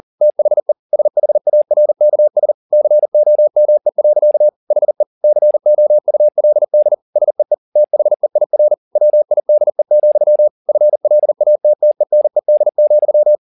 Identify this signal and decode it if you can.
Morse code, heard as a keyed tone, about 40 wpm, 600 Hz. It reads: THE SHARKS COME. HE COULD SEE THEIR WIDE, FLATTENED,
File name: the old man and the sea 2 40wpm